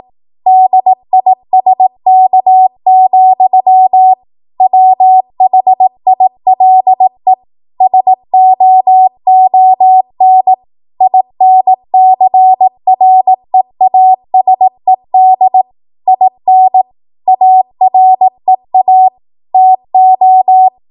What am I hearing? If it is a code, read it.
Morse code, heard as a keyed tone, about 18 wpm, 750 Hz. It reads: DISK, WHILE SOON INCREASED IN AREA TO
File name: Morse / 18WPM